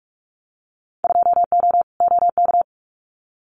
Morse code, heard as a keyed tone, about 44 wpm, 700 Hz. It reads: 2O OK
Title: Winter Field Day Exchanges Encore 45wpm